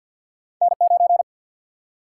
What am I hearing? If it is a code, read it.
Morse code, heard as a keyed tone, about 52 wpm, 700 Hz. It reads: N 9
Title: Single-Letter Single-Number 50wpm